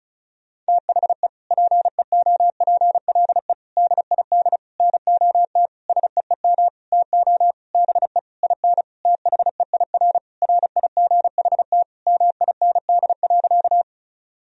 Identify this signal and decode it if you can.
Morse code, heard as a keyed tone, about 36 wpm, 700 Hz. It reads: THE PEOPLE DID NOT SEEM TO BE IN THEIR RIGHT MIND.
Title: Sentences from Top 1000 Words - Review Part 1 - 35wpm